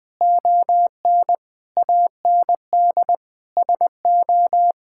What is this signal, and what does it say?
Morse code, heard as a keyed tone, about 20 wpm, 700 Hz. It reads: ON AND SO
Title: Sentences with 6-Word Phrases - 20wpm